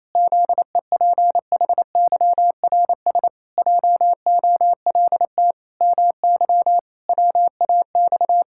Morse code, heard as a keyed tone, about 28 wpm, 700 Hz. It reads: ZEPHYRS JOLT MY WAX
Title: Pangram Challenge v2 - 28wpm